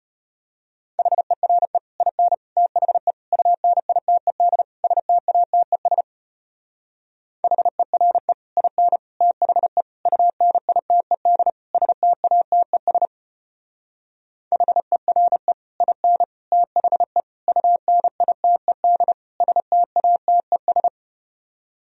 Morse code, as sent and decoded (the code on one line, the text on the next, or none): .... .-. / .. -. / - .... / ..- -. .. - . -.. / ... - .- - . ... / .... .-. / .. -. / - .... / ..- -. .. - . -.. / ... - .- - . ... / .... .-. / .. -. / - .... / ..- -. .. - . -.. / ... - .- - . ...
HERE IN THE UNITED STATES HERE IN THE UNITED STATES HERE IN THE UNITED STATES